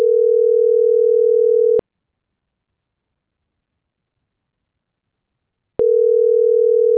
Ringback tone (North America), 440 + 480 Hz.